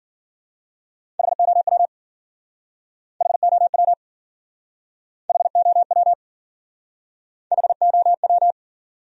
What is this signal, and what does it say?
Morse code, heard as a keyed tone, about 53 wpm, 700 Hz. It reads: HOW HOW HOW HOW